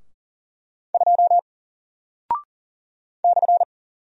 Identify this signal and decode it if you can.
Morse code, heard as a keyed tone, about 39 wpm, 700 Hz. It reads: 2 /
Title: Single Character (Letter:Number:Punctuation:Prosign) 40wpm